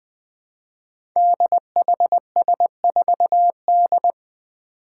Morse code, heard as a keyed tone, about 20 wpm, 700 Hz. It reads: DHS4D
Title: Five Challenging Letter-Number Combinations 20wpm